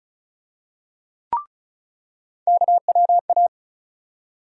Morse code, heard as a keyed tone, about 36 wpm, 700 Hz. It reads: KWA
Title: USA Call Sign Postfixes - 3 Letters 35wpm